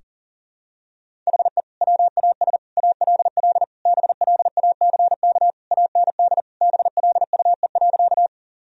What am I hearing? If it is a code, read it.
Morse code, heard as a keyed tone, about 37 wpm, 700 Hz. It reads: HE WAS ALL BLACK AND BLUE.
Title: Sentences from Top 1000 Words - Review Part 3 - 40wpm